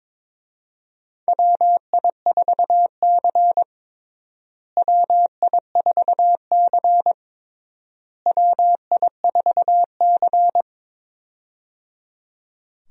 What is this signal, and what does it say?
Morse code, heard as a keyed tone, about 22 wpm, 700 Hz. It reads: WI4C WI4C WI4C
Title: USA Call Signs - 4 Characters - Repeated 3x - 22wpm